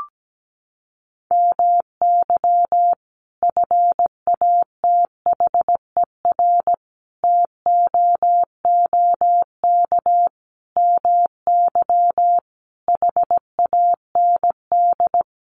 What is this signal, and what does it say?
Morse code, heard as a keyed tone, about 17 wpm, 700 Hz. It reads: MY FATHER TOOK MY HAND
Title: Sentences from Top 1500 Words 17wpm